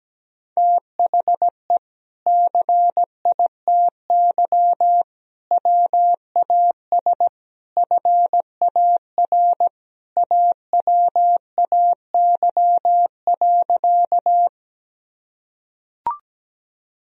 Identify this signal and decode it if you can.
Morse code, heard as a keyed tone, about 17 wpm, 700 Hz. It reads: THE CITY WAS FAR AWAY.
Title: Sentences from Top 1000 Words - Review Part 1 - 17wpm